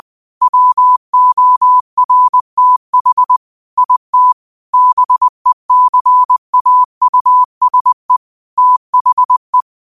Morse code, heard as a keyed tone, about 20 wpm, 1000 Hz. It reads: WORTH IT BECAUSE THE
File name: qotdfiles / 20wpm